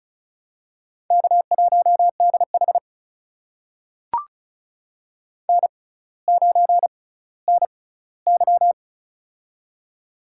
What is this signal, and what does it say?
Morse code, heard as a keyed tone, about 34 wpm, spaced as for 18 wpm, 700 Hz. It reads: K1DH N9NY
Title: USA Call Signs - 4 Characters - 6x Character Spacing - 35wpm